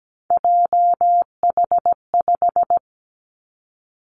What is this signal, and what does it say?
Morse code, heard as a keyed tone, about 17 wpm, 700 Hz. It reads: JH5
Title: Three Challenging Letter-Number Combinations 17wpm